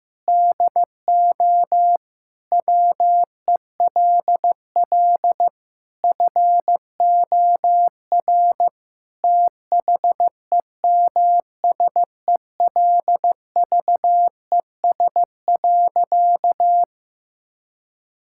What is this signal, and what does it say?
Morse code, heard as a keyed tone, about 15 wpm, 700 Hz. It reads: DO WELL FOR THEMSELVES.
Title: Sentences from Top 1000 Words - Review Part 5 - 15wpm